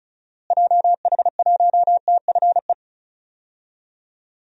Morse code, heard as a keyed tone, about 34 wpm, 700 Hz. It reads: JH1TFE
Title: DXCC-100 Call Signs - 6 Characters - Repeated 2x - 35wpm